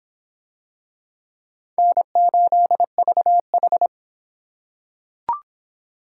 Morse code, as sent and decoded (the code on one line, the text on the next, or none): -. ---.. ...- ....
N8VH